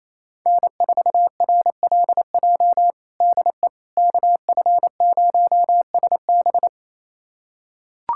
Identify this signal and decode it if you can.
Morse code, heard as a keyed tone, about 27 wpm, 700 Hz. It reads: N4RLJ DE KF0SB
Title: USA Callsign DE Callsign - 5 Characters - 28wpm